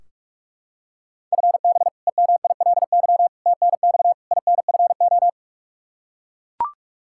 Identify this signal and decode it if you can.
Morse code, heard as a keyed tone, about 42 wpm, 700 Hz. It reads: FB EMILY TNX INFO
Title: Practice QSO - Exchange 3 - More Info - 45wpm